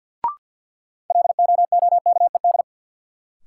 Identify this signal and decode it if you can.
Morse code, heard as a keyed tone, about 46 wpm, 700 Hz. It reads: LOOKED
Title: Sentences from Top 800 Words - New Word Drill - 50wpm